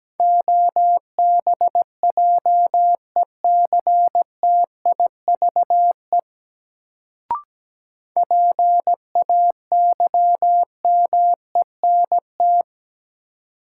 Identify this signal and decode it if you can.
Morse code, heard as a keyed tone, about 17 wpm, 700 Hz. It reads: OBJECTIVE PAYMENT
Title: Sentences from Top 1200 Words - New Word Drill Alt - 17wpm